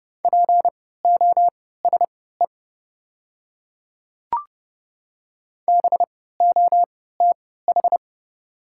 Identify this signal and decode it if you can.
Morse code, heard as a keyed tone, about 30 wpm, spaced as for 17 wpm, 690 Hz. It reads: POSE BOTH
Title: 3x Character Spacing - Top 500 Words - 30wpm